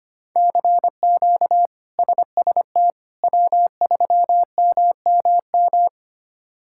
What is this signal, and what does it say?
Morse code, heard as a keyed tone, about 25 wpm, 700 Hz. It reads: CQ SST W3MMM
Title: K1USN SST QSO Practice - 25wpm